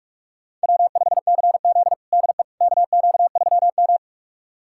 Morse code, heard as a keyed tone, about 46 wpm, 700 Hz. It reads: W5CZ DE KQ3M
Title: USA Callsign DE Callsign - 4 Characters - 45wpm